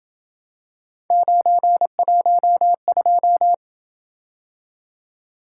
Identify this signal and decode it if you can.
Morse code, heard as a keyed tone, about 27 wpm, 700 Hz. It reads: 912